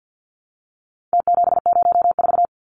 Morse code, heard as a keyed tone, about 49 wpm, 700 Hz. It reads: T704
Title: Numbers - Contest Style - 4-Digit 50wpm